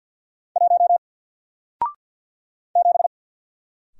Morse code, heard as a keyed tone, about 50 wpm, 700 Hz. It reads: 1 7
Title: Single Letter-Number Alt 50wpm